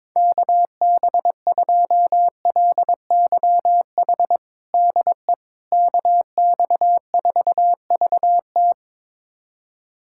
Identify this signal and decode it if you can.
Morse code, heard as a keyed tone, about 22 wpm, 700 Hz. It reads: KB2LYH DE KX4VT
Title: USA Callsign DE Callsign 22wpm